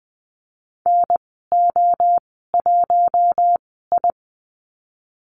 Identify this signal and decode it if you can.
Morse code, heard as a keyed tone, about 20 wpm, spaced as for 14 wpm, 700 Hz. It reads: NO1I
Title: 042 Lesson 8g - Number 1 - Call Signs 20wpm